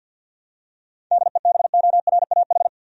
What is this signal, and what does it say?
Morse code, heard as a keyed tone, about 52 wpm, 700 Hz. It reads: DEBORAH
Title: USA Common Names - 50wpm